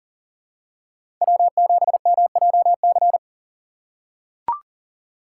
Morse code, heard as a keyed tone, about 40 wpm, 700 Hz. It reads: W7MJC
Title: USA Call Signs - 5 Characters - 6x Character Spacing - 40wpm